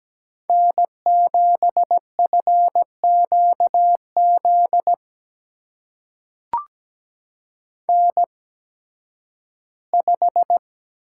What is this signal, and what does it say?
Morse code, heard as a keyed tone, about 17 wpm, spaced as for 11 wpm, 700 Hz. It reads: N7FQZ N5
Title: USA Call Signs - 5 Characters - 8x Character Spacing - 17wpm